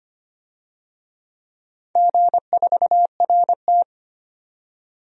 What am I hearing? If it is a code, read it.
Morse code, heard as a keyed tone, about 25 wpm, 700 Hz. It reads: G4RT
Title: DXCC-100 Call Signs - 4 Characters - 25wpm